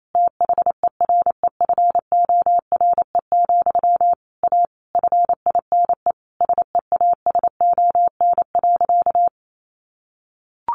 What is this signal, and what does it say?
Morse code, heard as a keyed tone, about 28 wpm, 700 Hz. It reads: THEREFORE, A FINE SEASON.
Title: Sentences from Top 1400 Words 28wpm